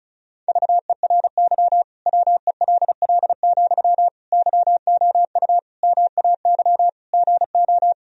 Morse code, heard as a keyed tone, about 36 wpm, 700 Hz. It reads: VERY WELL, YOU MAY GO
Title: Sentences from Top 1000 Words - Review Part 4 - 35wpm